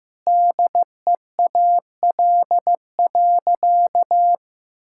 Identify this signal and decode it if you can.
Morse code, heard as a keyed tone, about 14 wpm, 700 Hz. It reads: DEAL.
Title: Sentences from Top 1000 Words - Review Part 6 - 15wpm